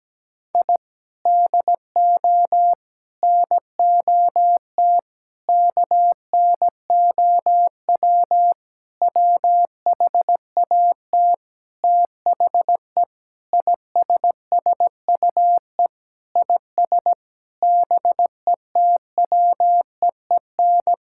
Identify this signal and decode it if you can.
Morse code, heard as a keyed tone, about 17 wpm, 700 Hz. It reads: I DO NOT KNOW WHAT THE ISSUE IS BETWEEN
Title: Sentences from Top 1000 Words - Review Part 6 - 17wpm